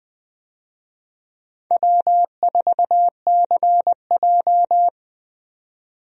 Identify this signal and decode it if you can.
Morse code, heard as a keyed tone, about 20 wpm, 700 Hz. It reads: W4CJ
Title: USA Call Signs - 4 Characters - 20wpm